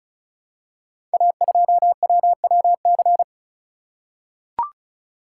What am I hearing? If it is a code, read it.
Morse code, heard as a keyed tone, about 36 wpm, 700 Hz. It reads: A2WWC
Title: Code Groups - 5 Letter:Numbers - 35wpm